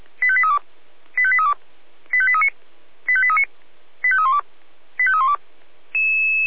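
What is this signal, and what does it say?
Five-tone selective call (ZVEI), heard as quick runs of five tones, tones of about 80 ms. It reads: 85832 85832 85839 85839 85312 85312
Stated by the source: ZVEI five-tone selective call received via SDR by Feuerwehr Orlamünde.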